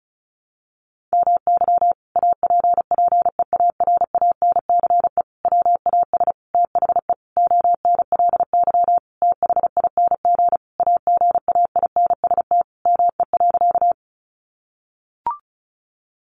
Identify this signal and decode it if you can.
Morse code, heard as a keyed tone, about 34 wpm, 700 Hz. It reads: MY APPEARANCE WAS THE ONLY THING AGAINST ME.
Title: Sentences from Top 1100 Words 35wpm